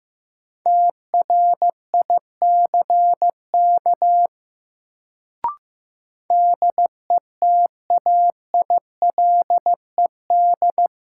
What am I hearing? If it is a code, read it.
Morse code, heard as a keyed tone, about 15 wpm, 700 Hz. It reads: TRICK DETAILED
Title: Sentences from Top 1500 Words - New Word Drill Alt - 15wpm